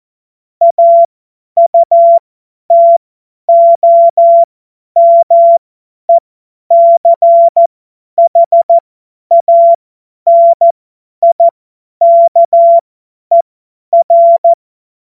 Morse code, heard as a keyed tone, about 14 wpm, spaced as for 10 wpm, 680 Hz. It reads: AUTOMECHANIKER
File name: Morsekurs / Witze / Witz 10.Oktober 2023 50 F70